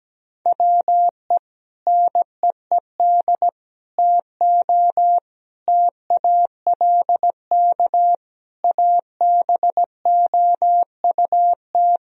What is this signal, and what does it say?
Morse code, heard as a keyed tone, about 17 wpm, 700 Hz. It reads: WE NEED TO TALK ABOUT
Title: Sentences from Top 1000 Words - Review Part 4 - 17wpm